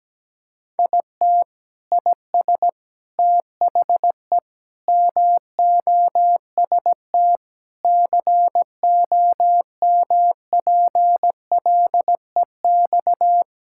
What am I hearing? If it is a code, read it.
Morse code, heard as a keyed tone, about 17 wpm, 700 Hz. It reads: IT IS THE MOST COMPLEX
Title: Sentences from Top 1000 Words - Review Part 6 - 17wpm